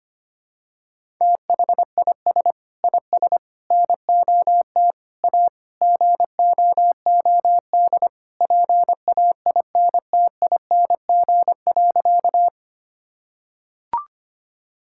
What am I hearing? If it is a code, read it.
Morse code, heard as a keyed tone, about 25 wpm, 700 Hz. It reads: THIS IS NOT A GOOD PAINTING.
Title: Sentences from Top 1000 Words - Review Part 6 - 25wpm